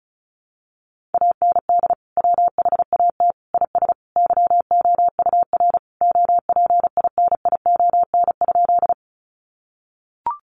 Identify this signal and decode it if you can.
Morse code, heard as a keyed tone, about 34 wpm, 700 Hz. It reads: AND WHAT IS YOUR OPINION?
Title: Sentences from Top 1200 Words 35wpm